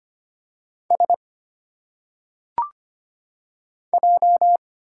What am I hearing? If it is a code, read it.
Morse code, heard as a keyed tone, about 25 wpm, 700 Hz. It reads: S J
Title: Single Letters - Phonetic Pronunciation 25wpm